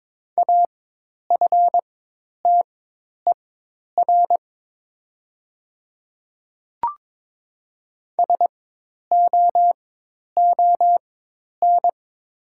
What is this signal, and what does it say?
Morse code, heard as a keyed tone, about 22 wpm, spaced as for 10 wpm, 700 Hz. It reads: AFTER SOON
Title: 4x Character Spacing - Top 500 Words - 22wpm